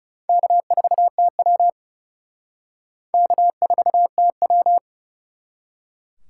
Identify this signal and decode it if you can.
Morse code, heard as a keyed tone, about 38 wpm, 700 Hz. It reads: K4TW K4TW